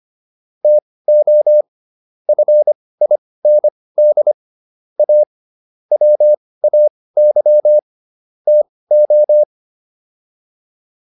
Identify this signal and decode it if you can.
Morse code, heard as a keyed tone, about 25 wpm, spaced as for 18 wpm, 600 Hz. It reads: TO FIND A WAY TO